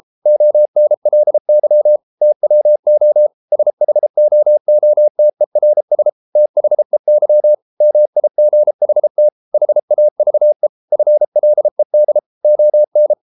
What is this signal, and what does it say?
Morse code, heard as a keyed tone, about 33 wpm, 600 Hz. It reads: ONLY TWO SHOOTERS THEY MIGHT HAVE FLED ON